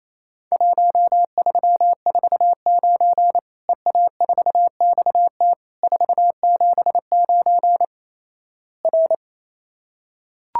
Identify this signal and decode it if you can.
Morse code, heard as a keyed tone, about 27 wpm, 700 Hz. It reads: 1349 EA4XT 479 R
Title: Worked All Europe DX Contest - QTC Exchange - 28wpm